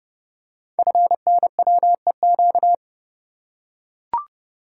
Morse code, heard as a keyed tone, about 29 wpm, 700 Hz. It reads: FNWEQ
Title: Code Groups - 5 Letters - 30wpm